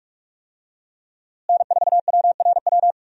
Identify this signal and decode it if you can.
Morse code, heard as a keyed tone, about 41 wpm, 700 Hz. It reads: N4WRW
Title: USA Call Signs - 5 Characters - Repeated 3x - 45wpm